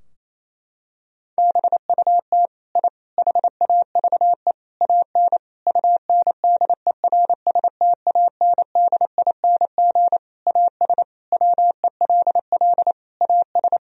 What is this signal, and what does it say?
Morse code, heard as a keyed tone, about 28 wpm, 700 Hz. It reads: BUT I HAVE AN UNDERSTANDING AS WELL AS
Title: Sentences from Top 1000 Words - Review Part 3 - 28wpm